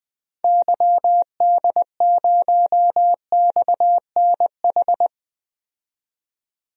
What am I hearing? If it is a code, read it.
Morse code, heard as a keyed tone, about 21 wpm, 700 Hz. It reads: YD0XNH
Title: DXCC-100 Call Signs - 6 Characters - 20wpm